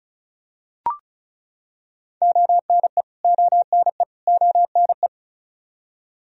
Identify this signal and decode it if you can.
Morse code, heard as a keyed tone, about 34 wpm, 700 Hz. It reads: ONE ONE ONE
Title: Top 100 Words Repeated 35wpm